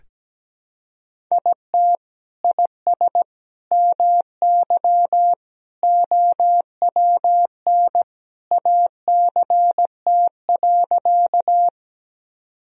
Morse code, heard as a keyed tone, about 17 wpm, 700 Hz. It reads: IT IS MY OWN ACT.
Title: Sentences from Top 1000 Words - Review Part 1 - 17wpm